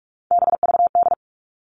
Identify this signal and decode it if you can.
Morse code, heard as a keyed tone, about 44 wpm, 700 Hz. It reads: BVD
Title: All Cut Numbers - 3-Digit 45wpm